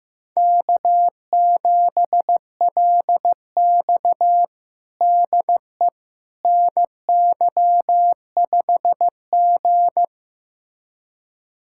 Morse code, heard as a keyed tone, about 15 wpm, 700 Hz. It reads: K7LX DE NY5G
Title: USA Callsign DE Callsign - 4 Characters - 15wpm